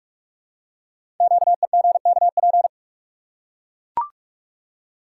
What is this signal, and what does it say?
Morse code, heard as a keyed tone, about 44 wpm, 700 Hz. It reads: QEGKP
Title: Code Groups - 5 Letters - 45wpm